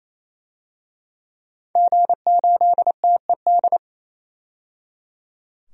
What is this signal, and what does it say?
Morse code, heard as a keyed tone, about 28 wpm, 700 Hz. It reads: G8TED
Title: DXCC-100 Call Signs - 5 Characters - 28wpm